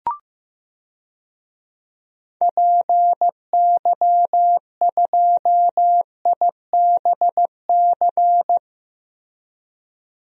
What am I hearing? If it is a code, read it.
Morse code, heard as a keyed tone, about 15 wpm, 700 Hz. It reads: PY2IBC
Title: DXCC-100 Call Signs - 6 Characters - Repeated 3x - 15wpm